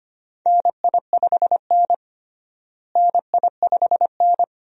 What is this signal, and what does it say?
Morse code, heard as a keyed tone, about 25 wpm, 700 Hz. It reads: NI5N NI5N
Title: USA Call Signs - 4 Characters - Repeated 2x - 25wpm